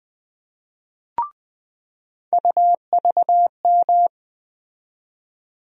Morse code, heard as a keyed tone, about 20 wpm, 700 Hz. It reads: UVM
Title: USA Call Sign Postfixes - 3 Letters 20wpm